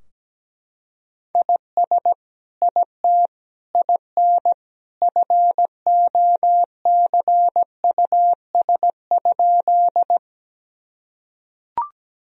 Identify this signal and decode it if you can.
Morse code, heard as a keyed tone, about 17 wpm, 700 Hz. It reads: IS IT IN FOCUS?
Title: Sentences from Top 1000 Words - Review Part 4 - 17wpm